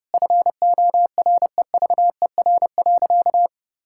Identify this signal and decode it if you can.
Morse code, heard as a keyed tone, about 30 wpm, 700 Hz. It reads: FOREVER.
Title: Sentences from Top 1600 Words 30wpm